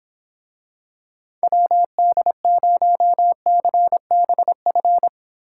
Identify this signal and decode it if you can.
Morse code, heard as a keyed tone, about 23 wpm, 700 Hz. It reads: WD0CBF